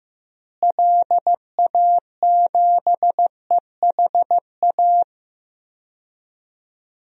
Morse code, heard as a keyed tone, about 15 wpm, 700 Hz. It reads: LA7EHA
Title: DXCC-100 Call Signs - 6 Characters - Repeated 3x - 15wpm